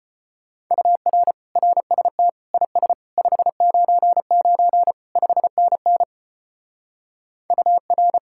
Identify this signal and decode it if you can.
Morse code, heard as a keyed tone, about 33 wpm, 700 Hz. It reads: UR RST IS 599 5NN UR